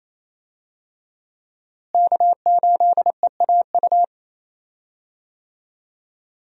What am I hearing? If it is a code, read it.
Morse code, heard as a keyed tone, about 28 wpm, 700 Hz. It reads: K8EAU
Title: USA Call Signs - 5 Characters - Repeated 2x - 28wpm